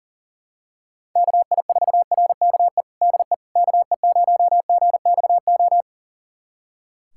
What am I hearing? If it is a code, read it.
Morse code, heard as a keyed tone, about 37 wpm, 700 Hz. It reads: KI4RKE DE KE0GXO
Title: USA Callsign DE Callsign - 6 Characters - 40wpm